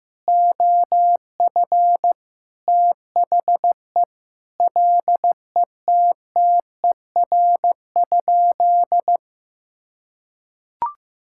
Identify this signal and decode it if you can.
Morse code, heard as a keyed tone, about 15 wpm, 700 Hz. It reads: OF THE LETTER?
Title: Sentences from Top 1000 Words - Review Part 6 - 15wpm